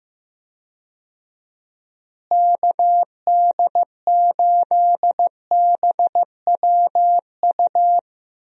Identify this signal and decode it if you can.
Morse code, heard as a keyed tone, about 15 wpm, 700 Hz. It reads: KD8BWU
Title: USA Call Signs - 6 Characters - Repeated 2x - 15wpm